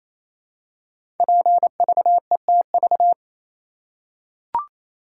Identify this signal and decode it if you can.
Morse code, heard as a keyed tone, about 27 wpm, 700 Hz. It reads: PVETV
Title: Code Groups - 5 Letters - 28wpm